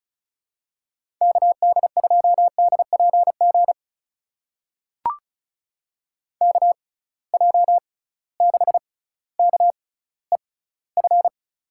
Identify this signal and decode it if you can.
Morse code, heard as a keyed tone, about 34 wpm, spaced as for 18 wpm, 700 Hz. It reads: KD2DPG KJ6KEF
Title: USA Call Signs - 6 Characters - 6x Character Spacing - 35wpm